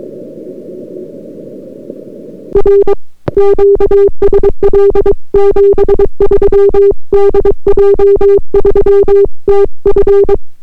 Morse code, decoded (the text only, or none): R QSL 73 DJ3TF